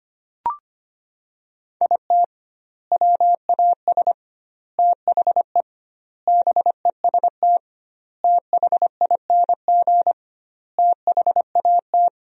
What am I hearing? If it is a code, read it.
Morse code, heard as a keyed tone, about 25 wpm, 700 Hz. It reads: IT WAS THE BEST THING THAT
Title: 2x Word-Spacing - Sentences from Top 400 Words - 25wpm